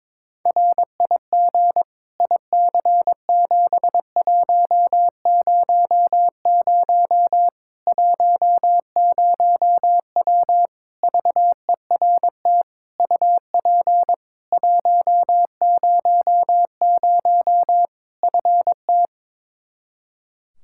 Morse code, heard as a keyed tone, about 22 wpm, 700 Hz. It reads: RIG IC7100 10W VERT UP 100 FT
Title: Practice QSO - Exchange 2 - Info - 22wpm